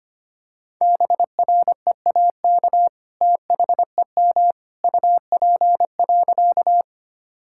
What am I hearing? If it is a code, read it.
Morse code, heard as a keyed tone, about 25 wpm, 700 Hz. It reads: BREAK THEM UP.
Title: Sentences from Top 1000 Words - Review Part 6 - 25wpm